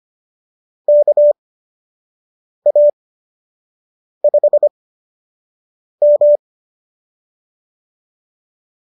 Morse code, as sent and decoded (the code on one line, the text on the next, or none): -.- .- ..... --
KA5M